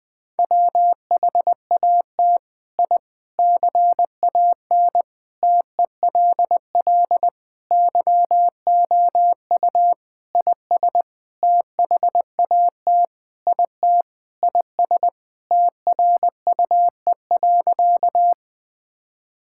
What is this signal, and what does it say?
Morse code, heard as a keyed tone, about 20 wpm, 700 Hz. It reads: WHAT I CAN TELL YOU IS THAT IT IS TRUE.
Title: Sentences with 7-Word Phrases - 20wpm